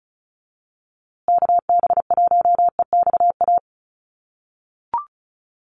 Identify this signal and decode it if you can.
Morse code, heard as a keyed tone, about 34 wpm, 700 Hz. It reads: KB1EXA
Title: USA Call Signs Repeated 3x 35wpm